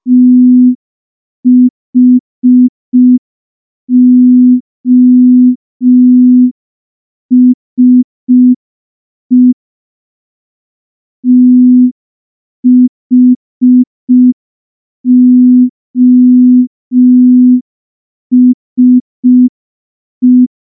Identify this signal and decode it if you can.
Morse code, heard as a keyed tone, about 5 wpm, 250 Hz. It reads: THOSE THOSE
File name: morse / morse-training-05WPM